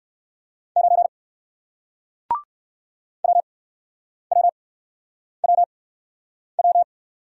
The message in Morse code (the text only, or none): 9 WWWW